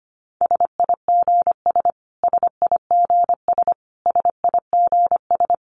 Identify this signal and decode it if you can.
Morse code, heard as a keyed tone, about 25 wpm, 700 Hz. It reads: SIGS SIGS SIGS
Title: Abbreviations Repeated 3x - 25wpm